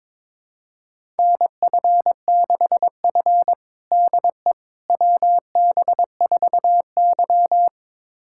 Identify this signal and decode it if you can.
Morse code, heard as a keyed tone, about 22 wpm, 700 Hz. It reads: NF6F DE WB4Y
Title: USA Callsign DE Callsign - 4 Characters - 22wpm